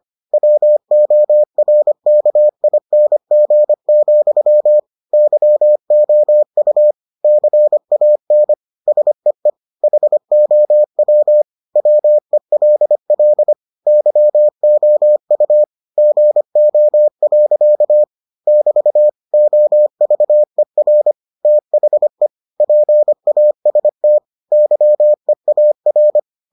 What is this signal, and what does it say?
Morse code, heard as a keyed tone, about 25 wpm, 600 Hz. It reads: WORKING, YOU CAN SEE HOW WELL YOU GO. <BT> OVER THE PAST YEAR